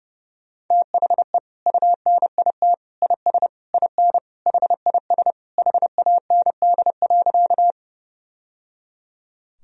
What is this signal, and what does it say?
Morse code, heard as a keyed tone, about 30 wpm, 700 Hz. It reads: THE UNIT IS IN HIS HAND.
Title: Sentences from Top 1000 Words - Review Part 2 - 30wpm